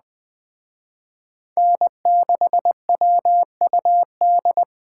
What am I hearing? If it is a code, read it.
Morse code, heard as a keyed tone, about 21 wpm, 700 Hz. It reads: N6WUD
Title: Speed Racing - USA 1x3 Call Signs 17wpm